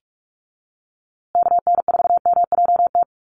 Morse code, heard as a keyed tone, about 44 wpm, 700 Hz. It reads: KN4MJT
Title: USA Call Signs Repeated 3x 45wpm